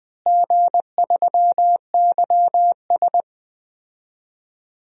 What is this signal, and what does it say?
Morse code, heard as a keyed tone, about 20 wpm, 700 Hz. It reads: G3YS
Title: DXCC-100 Call Signs - 4 Characters - Repeated 2x - 20wpm